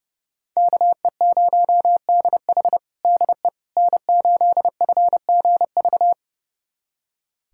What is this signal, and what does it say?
Morse code, heard as a keyed tone, about 32 wpm, 700 Hz. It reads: KE0DH DE N8FGV
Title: USA Callsign DE Callsign - 5 Characters - 30wpm